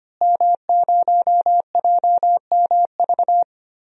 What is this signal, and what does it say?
Morse code, heard as a keyed tone, about 25 wpm, 700 Hz. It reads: M0JMV
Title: DXCC-100 Call Signs - 5 Characters - Repeated 3x - 25wpm